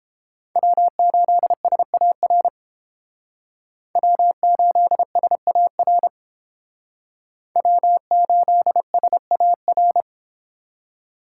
Morse code, heard as a keyed tone, about 28 wpm, 700 Hz. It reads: W8SAR W8SAR W8SAR